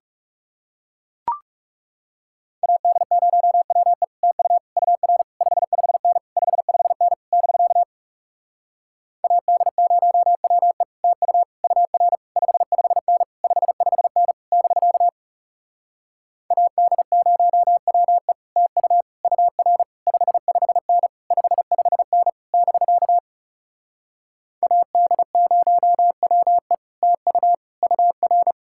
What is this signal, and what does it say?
Morse code, heard as a keyed tone, about 40 wpm, 700 Hz. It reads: AD0WE TU UR 55N 55N <BK> AD0WE TU UR 55N 55N <BK> AD0WE TU UR 55N 55N <BK> AD0WE TU UR